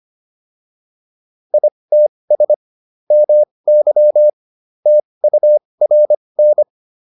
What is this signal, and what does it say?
Morse code, heard as a keyed tone, about 25 wpm, spaced as for 20 wpm, 600 Hz. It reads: ITS MY TURN
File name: PR302 20